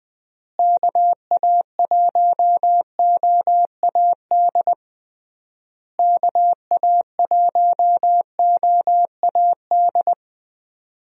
Morse code, heard as a keyed tone, about 20 wpm, 700 Hz. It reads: KA1OAD KA1OAD
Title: USA Call Signs - 6 Characters - Repeated 3x - 20wpm